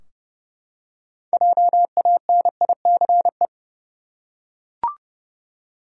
Morse code, heard as a keyed tone, about 30 wpm, 700 Hz. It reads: JANICE
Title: USA Common Names - 30wpm